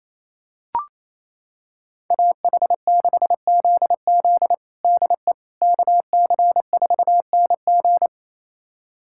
Morse code, heard as a keyed tone, about 27 wpm, 700 Hz. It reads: AH6ZZ DE KC4NG
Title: USA Callsign DE Callsign - 5 Characters - 28wpm